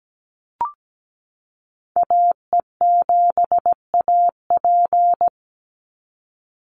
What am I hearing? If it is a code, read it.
Morse code, heard as a keyed tone, about 17 wpm, 700 Hz. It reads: AE7AP
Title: NA SOTA Chaser Callsigns 17wpm